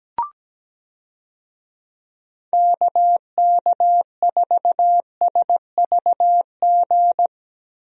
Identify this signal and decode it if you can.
Morse code, heard as a keyed tone, about 17 wpm, 700 Hz. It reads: KK4SVG
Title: USA Call Signs - 6 Characters - Repeated 3x - 17wpm